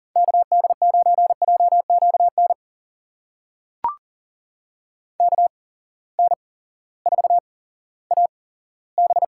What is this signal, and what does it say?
Morse code, heard as a keyed tone, about 40 wpm, spaced as for 18 wpm, 700 Hz. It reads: KD9JQN KN4AB
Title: USA Call Signs - 6 Characters - 8x Character Spacing - 40wpm